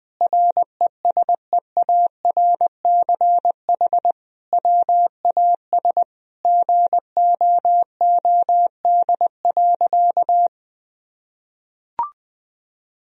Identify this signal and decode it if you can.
Morse code, heard as a keyed tone, about 21 wpm, 700 Hz. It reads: RESEARCH WAS GOOD.
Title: Sentences from Top 1000 Words - Review Part 4 - 20wpm